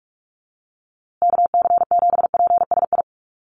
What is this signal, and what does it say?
Morse code, heard as a keyed tone, about 44 wpm, 700 Hz. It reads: KC7PSI
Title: USA Call Signs Repeated 3x 45wpm